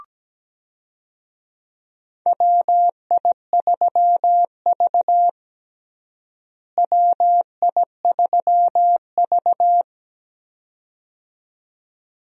Morse code, heard as a keyed tone, about 17 wpm, 700 Hz. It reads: WI3V WI3V
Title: USA Call Signs - 4 Characters - Repeated 2x - 17wpm